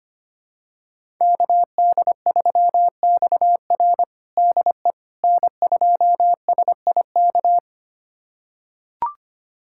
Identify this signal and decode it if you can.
Morse code, heard as a keyed tone, about 25 wpm, 700 Hz. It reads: KD3XR DE N2SIK
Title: USA Callsign DE Callsign - 5 Characters - 25wpm